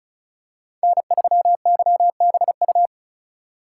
Morse code, heard as a keyed tone, about 36 wpm, 700 Hz. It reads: N3YBU
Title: USA Call Signs - 5 Characters - 3x Character Spacing - 35wpm